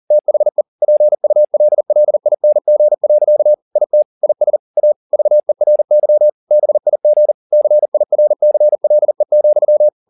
Morse code, heard as a keyed tone, about 40 wpm, 600 Hz. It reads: THE PULLING. IT IS A VERY BIG CIRCLE,
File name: the old man and the sea 2 40wpm